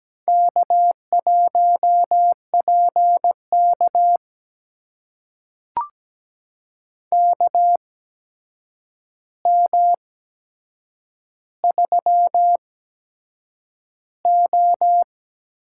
Morse code, heard as a keyed tone, about 17 wpm, spaced as for 7 wpm, 700 Hz. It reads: K1PK KM3O
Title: USA Call Signs - 4 Characters - 8x Character Spacing - 17wpm